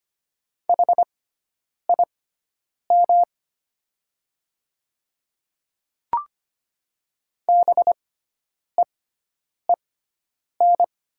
Morse code, heard as a keyed tone, about 25 wpm, spaced as for 9 wpm, 700 Hz. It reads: HIM BEEN
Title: 6x Character Spacing - Top 200 Words - 25wpm